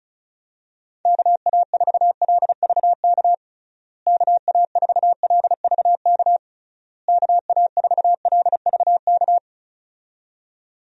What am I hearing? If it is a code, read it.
Morse code, heard as a keyed tone, about 36 wpm, 700 Hz. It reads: KA4LVK KA4LVK KA4LVK
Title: USA Call Signs - 6 Characters - Repeated 3x - 35wpm